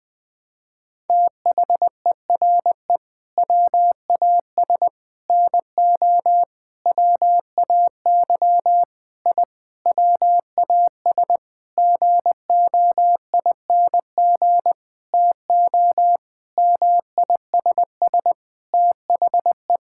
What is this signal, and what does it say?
Morse code, heard as a keyed tone, about 20 wpm, 700 Hz. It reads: THERE WAS NO WAY I WAS GOING TO MISS THE
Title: Sentences with 8-Word Phrases - 20wpm